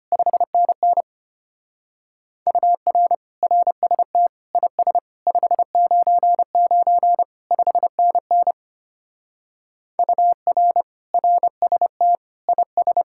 Morse code, heard as a keyed tone, about 30 wpm, 700 Hz. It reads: 5NN UR RST IS 599 5NN UR RST IS
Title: Speed-Racing Common QSO Phrases - 25wpm